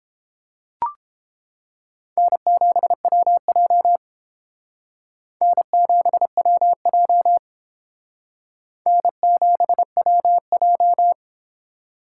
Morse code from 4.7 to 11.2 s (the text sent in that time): N7WJ N7WJ